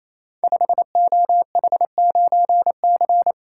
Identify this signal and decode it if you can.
Morse code, heard as a keyed tone, about 27 wpm, 690 Hz. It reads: 5OH9C
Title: Code Groups - 5 Letter:Numbers - 28wpm